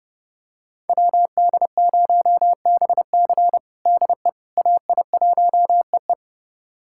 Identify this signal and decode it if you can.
Morse code, heard as a keyed tone, about 30 wpm, 700 Hz. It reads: WD0BC DE AI1EE
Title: USA Callsign DE Callsign - 5 Characters - 30wpm